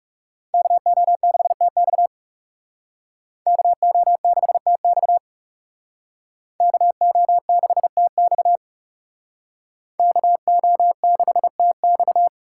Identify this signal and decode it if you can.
Morse code, heard as a keyed tone, about 35 wpm, 700 Hz. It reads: KO6TX KO6TX KO6TX KO6TX
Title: Speed Racing - USA 2x2 Call Signs 30wpm